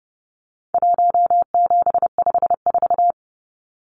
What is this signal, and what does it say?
Morse code, heard as a keyed tone, about 30 wpm, 700 Hz. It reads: 1754
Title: Numbers - Contest Style - 4-Digit 30wpm